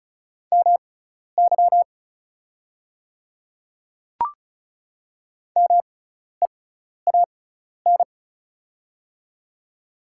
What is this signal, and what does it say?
Morse code, heard as a keyed tone, about 36 wpm, spaced as for 12 wpm, 700 Hz. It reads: MY MEAN
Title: 6x Character Spacing - Top 200 Words - 35wpm